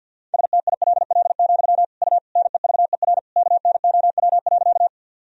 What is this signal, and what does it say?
Morse code, heard as a keyed tone, about 51 wpm, 700 Hz. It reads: STILL, U NEVER KNOW.
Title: Sentences from Top 300 Words and Abbreviations 50wpm